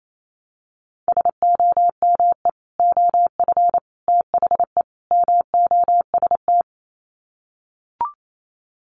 Morse code, heard as a keyed tone, about 28 wpm, 700 Hz. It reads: SOME OF THE MOST
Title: Sets of 4 Words 28wpm